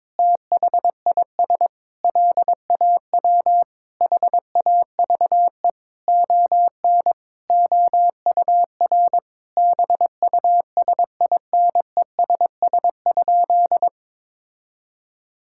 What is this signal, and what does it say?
Morse code, heard as a keyed tone, about 22 wpm, 700 Hz. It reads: THIS LAW HAVE ON OUR BUSINESS?
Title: Sentences from Top 1000 Words - Review Part 6 - 22wpm